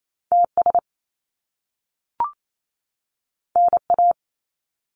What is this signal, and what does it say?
Morse code, heard as a keyed tone, about 28 wpm, 700 Hz. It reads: TS NA
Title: Top Two Letter Combinations in Words 28wpm